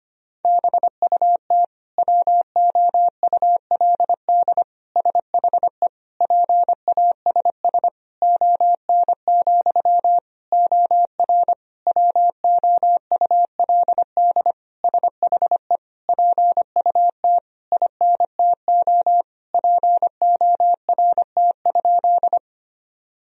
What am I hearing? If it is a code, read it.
Morse code, heard as a keyed tone, about 25 wpm, 700 Hz. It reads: BUT WOULD SHE PASS ON, OR WOULD SHE PUT INTO PORT?
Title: Sentences from Top 1000 Words - Review Part 1 - 25wpm